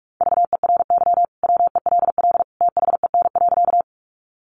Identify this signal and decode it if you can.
Morse code, heard as a keyed tone, about 44 wpm, 700 Hz. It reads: VERY WELL THEN.
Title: Sentences from Top 200 Words 45wpm